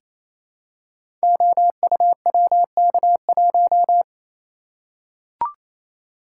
Morse code, heard as a keyed tone, about 29 wpm, 700 Hz. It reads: OUWK1
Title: Code Groups - 5 Letter:Numbers - 28wpm